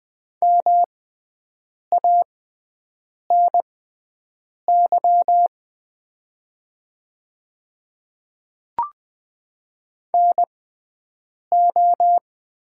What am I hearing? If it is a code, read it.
Morse code, heard as a keyed tone, about 20 wpm, spaced as for 7 wpm, 700 Hz. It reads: MANY NO
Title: 6x Character Spacing - Top 200 Words - 20wpm